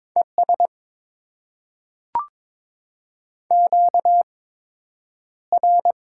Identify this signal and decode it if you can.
Morse code, heard as a keyed tone, about 22 wpm, spaced as for 9 wpm, 700 Hz. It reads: ES QR
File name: Top 20 QSO Elements - 8x Character Spacing - 22wpm